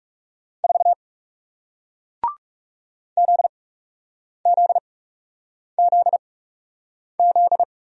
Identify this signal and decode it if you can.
Morse code, heard as a keyed tone, about 37 wpm, spaced as for 8 wpm, 700 Hz. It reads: 4 ZZZZ
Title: Speed-Racing Single Letter-Number 30wpm